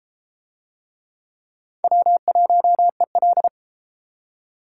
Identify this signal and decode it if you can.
Morse code, heard as a keyed tone, about 32 wpm, 700 Hz. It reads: W1EL